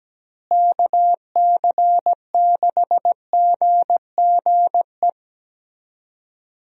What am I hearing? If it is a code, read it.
Morse code, heard as a keyed tone, about 17 wpm, 700 Hz. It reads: KC6GGE
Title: USA Call Signs - 6 Characters - Repeated 3x - 17wpm